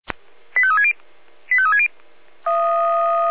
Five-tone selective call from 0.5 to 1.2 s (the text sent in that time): 85360